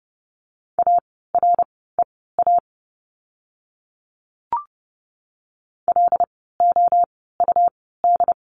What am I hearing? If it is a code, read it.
Morse code, heard as a keyed tone, about 30 wpm, spaced as for 17 wpm, 700 Hz. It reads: AREA LOUD
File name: Lesson 17f - Letter u - Words 30wpm